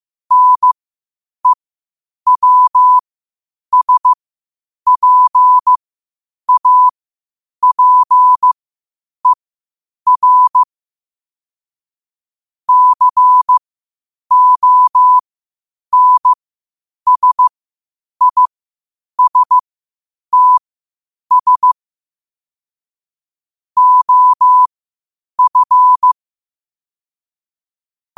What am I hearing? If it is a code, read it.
Morse code, heard as a keyed tone, about 15 wpm, spaced as for 8 wpm, 1000 Hz. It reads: NEWSPAPER CONSISTS OF
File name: qotdfiles / 8wpm